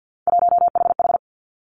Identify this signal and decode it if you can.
Morse code, heard as a keyed tone, about 50 wpm, 700 Hz. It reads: 1HH
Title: Three Challenging Letter-Number Combinations 50wpm